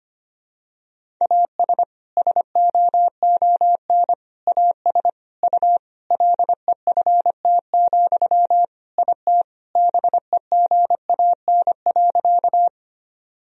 Morse code, heard as a keyed tone, about 25 wpm, 700 Hz. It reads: AS SOON AS U LEFT, IT BEGAN.
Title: Sentences from Top 400 Words and Abbreviations 25wpm